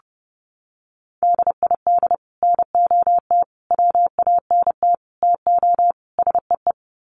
Morse code, heard as a keyed tone, about 30 wpm, 700 Hz. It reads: DID NOT WANT TO SEE
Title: Sets of 5 Words Encore2 30wpm